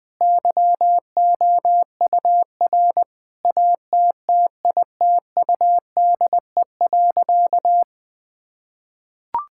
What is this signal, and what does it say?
Morse code, heard as a keyed tone, about 20 wpm, 700 Hz. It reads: YOUR ATTITUDE.
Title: Sentences with 5-Word Phrases - 20wpm